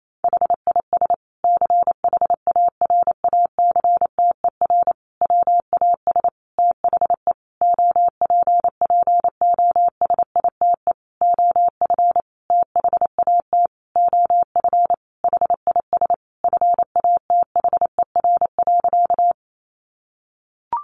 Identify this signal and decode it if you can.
Morse code, heard as a keyed tone, about 28 wpm, 700 Hz. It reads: HIS CHARACTER WAS THE OPPOSITE OF THAT OF HIS FATHER.
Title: Sentences from Top 1300 Words 28wpm